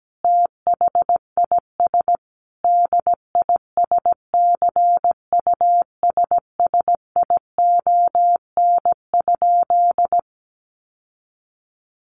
Morse code, heard as a keyed tone, about 17 wpm, 700 Hz. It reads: THIS DISCUSSION?
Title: Sentences from Top 1100 Words 17wpm